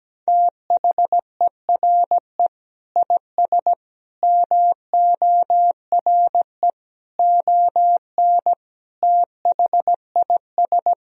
Morse code, heard as a keyed tone, about 17 wpm, 700 Hz. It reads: THERE IS MORE ON THIS
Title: Sentences from Top 1000 Words - Review Part 1 - 17wpm